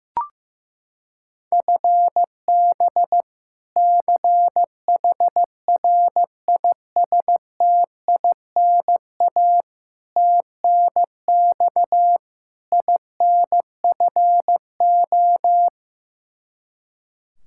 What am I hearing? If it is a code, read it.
Morse code, heard as a keyed tone, about 15 wpm, 700 Hz. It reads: FB CHRISTINA TNX INFO
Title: Practice QSO - Exchange 3 - More Info - 15wpm